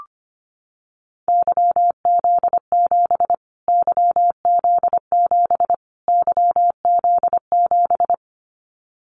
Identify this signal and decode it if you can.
Morse code, heard as a keyed tone, about 25 wpm, 700 Hz. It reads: YZ7 YZ7 YZ7
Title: Callsign Prefixes for Top 25 DXCC - Repeated 3x - 25wpm